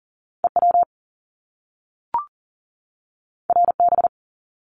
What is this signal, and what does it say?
Morse code, heard as a keyed tone, about 41 wpm, 700 Hz. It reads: EW RB
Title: Top Two Letter Combinations in Words 40wpm